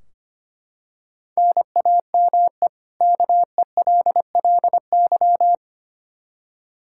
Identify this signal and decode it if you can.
Morse code, heard as a keyed tone, about 25 wpm, 700 Hz. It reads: NAME KELLY
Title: K1USN SST QSO Practice 25wpm